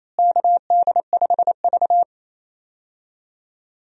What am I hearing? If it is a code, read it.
Morse code, heard as a keyed tone, about 27 wpm, 700 Hz. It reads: KD5V
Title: USA Call Signs - 4 Characters - Repeated 3x - 28wpm